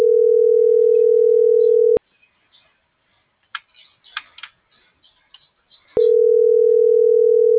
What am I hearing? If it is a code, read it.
Ringback tone (North America), 440 + 480 Hz.